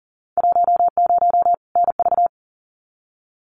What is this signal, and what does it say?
Morse code, heard as a keyed tone, about 38 wpm, 700 Hz. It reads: 10 NV
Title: Winter Field Day Exchanges 40wpm